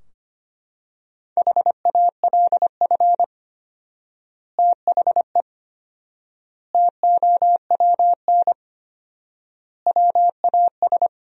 Morse code, heard as a keyed tone, about 25 wpm, 700 Hz. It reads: HALF THE TOWN WAS
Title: 4x Word-Spacing - Sentences from Top 500 Words - 25wpm